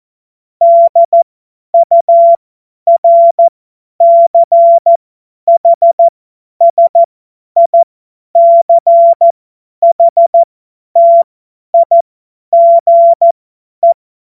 Morse code, heard as a keyed tone, about 14 wpm, spaced as for 10 wpm, 680 Hz. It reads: DURCHSICHTIGE
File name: Witz-21-Oktober-2025-50-F70